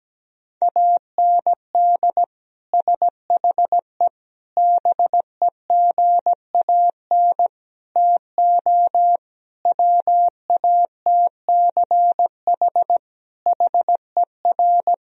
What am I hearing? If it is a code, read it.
Morse code, heard as a keyed tone, about 17 wpm, 700 Hz. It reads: AND SHE BEGAN TO WATCH HER
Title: Sentences from Top 1000 Words - Review Part 1 - 17wpm